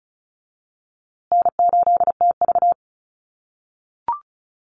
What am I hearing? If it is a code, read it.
Morse code, heard as a keyed tone, about 34 wpm, 700 Hz. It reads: N8TV
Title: USA Call Signs Repeated 3x 35wpm